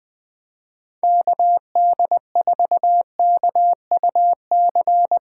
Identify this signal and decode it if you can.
Morse code, heard as a keyed tone, about 20 wpm, 700 Hz. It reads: KD4KUC
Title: USA Callsign DE Callsign 20wpm